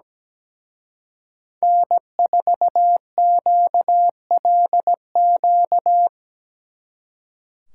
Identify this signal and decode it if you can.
Morse code, heard as a keyed tone, about 20 wpm, 700 Hz. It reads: N4QLQ
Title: Speed Racing - USA 1x3 Call Signs 17wpm